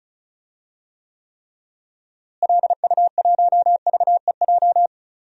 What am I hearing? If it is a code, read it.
Morse code, heard as a keyed tone, about 36 wpm, 700 Hz. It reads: LU1VEJ
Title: DXCC-100 Call Signs - 6 Characters - Repeated 3x - 35wpm